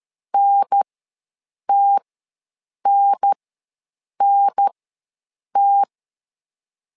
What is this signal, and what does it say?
Morse code, heard as a keyed tone, about 13 wpm, spaced as for 7 wpm, 800 Hz. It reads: NTNNT
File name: LEZIONI / 2 UHDN / 2 UHDN ws 12wpm 1708